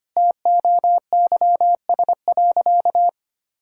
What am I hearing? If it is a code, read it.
Morse code, heard as a keyed tone, about 25 wpm, 700 Hz. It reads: TOYS.
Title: Pangram Challenge v2 - 25wpm